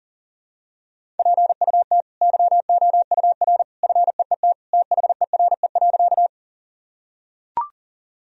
Morse code, heard as a keyed tone, about 40 wpm, 700 Hz. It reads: PUT YOUR FEET THERE.
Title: Sentences from Top 1000 Words - Review Part 1 - 40wpm